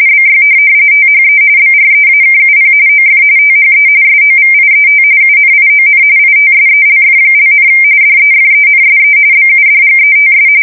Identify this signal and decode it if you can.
RTTY, about 45.45 baud. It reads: BY THE BREAKUP OF THE SOVIET UNION ITSELF), DEFENSE BUDGETS
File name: fsk170 / USS-New-Jersey